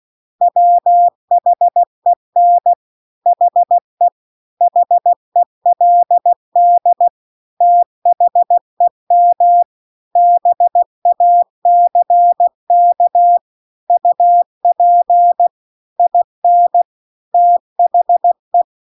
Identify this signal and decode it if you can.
Morse code, heard as a keyed tone, about 16 wpm, 700 Hz. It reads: WHEN HE HELD THEM BACK UP IN THE